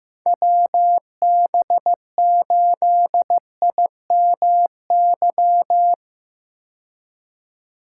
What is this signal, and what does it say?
Morse code, heard as a keyed tone, about 15 wpm, 700 Hz. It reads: WB8IMY
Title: K1USN SST QSO Practice - 15wpm